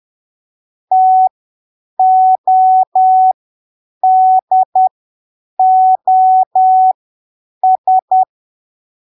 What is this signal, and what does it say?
Morse code, heard as a keyed tone, about 10 wpm, spaced as for 7 wpm, 750 Hz. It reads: TODOS
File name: arquivos / texto / texto 25 10ppm